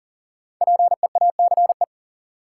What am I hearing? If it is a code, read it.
Morse code, heard as a keyed tone, about 39 wpm, 700 Hz. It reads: PEACE
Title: Sentences from Top 1500 Words - New Word Drill Alt - 40wpm